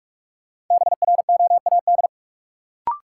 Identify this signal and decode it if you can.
Morse code, heard as a keyed tone, about 46 wpm, 700 Hz. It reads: BROAD
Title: Sentences from Top 1000 Words - New Word Drill - 45wpm